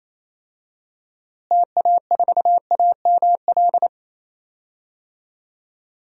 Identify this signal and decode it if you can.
Morse code, heard as a keyed tone, about 28 wpm, 700 Hz. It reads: TA4AML
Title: DXCC-100 Call Signs - 6 Characters - Repeated 3x - 28wpm